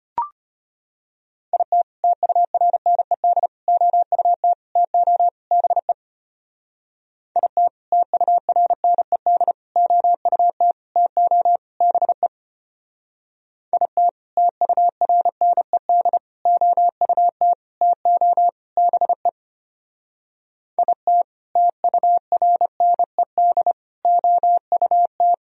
Morse code, heard as a keyed tone, about 32 wpm, 700 Hz. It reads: IT TURNED OUT TO BE IT TURNED OUT TO BE IT TURNED OUT TO BE IT TURNED OUT